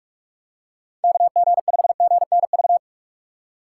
Morse code, heard as a keyed tone, about 44 wpm, 700 Hz. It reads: KG5GNV
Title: USA Call Signs - 6 Characters - 6x Character Spacing - 45wpm